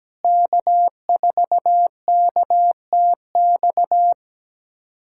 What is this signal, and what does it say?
Morse code, heard as a keyed tone, about 17 wpm, 690 Hz. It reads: K4KTX
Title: SOTA QSO Practice 17wpm